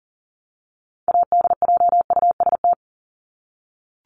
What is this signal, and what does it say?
Morse code, heard as a keyed tone, about 40 wpm, 700 Hz. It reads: ADJUST
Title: Top 1501-1750 Words 40wpm